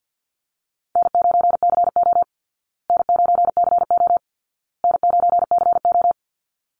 Morse code, heard as a keyed tone, about 50 wpm, 700 Hz. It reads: N9CO N9CO N9CO
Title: Call Signs Repeated 2x - 50wpm